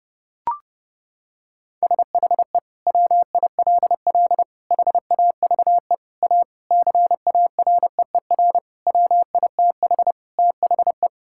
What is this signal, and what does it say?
Morse code, heard as a keyed tone, about 30 wpm, 700 Hz. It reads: SHE WILL HAVE A CAREER WITH THE
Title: Sentences from Top 1000 Words - Review Part 6 - 30wpm